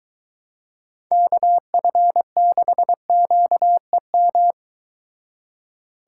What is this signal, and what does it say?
Morse code, heard as a keyed tone, about 22 wpm, 700 Hz. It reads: KF6QEM